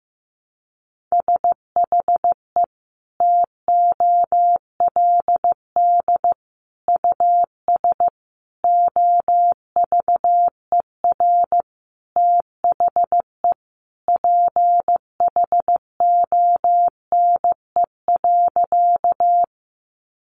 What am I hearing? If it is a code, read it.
Morse code, heard as a keyed tone, about 14 wpm, 700 Hz. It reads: SHE TOLD US OVER THE PHONE.
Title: Sentences from Top 1100 Words 15wpm